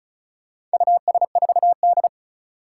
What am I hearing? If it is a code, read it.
Morse code, heard as a keyed tone, about 36 wpm, 700 Hz. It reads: US4D
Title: Four Challenging Letter-Number Combinations 35wpm